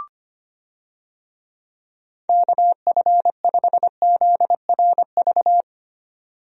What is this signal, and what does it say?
Morse code, heard as a keyed tone, about 25 wpm, 700 Hz. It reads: KF5ZRV
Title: USA Call Signs - 6 Characters - 25wpm